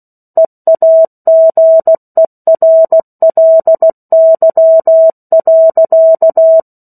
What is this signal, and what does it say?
Morse code, heard as a keyed tone, about 16 wpm, 650 Hz. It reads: EAGERLY.
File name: 03-Chapter-03 16 WPM